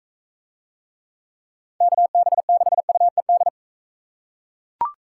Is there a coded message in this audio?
Yes: Morse code from 1.8 to 3.5 s, heard as a keyed tone, about 39 wpm, 700 Hz. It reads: KB6UED